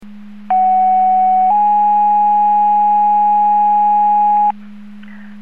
Two-tone page, heard as a one-second tone, then a three-second tone, about 750 then 830 Hz. It is heard from 0.5 to 4.5 s.
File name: Portage 154.385 Paging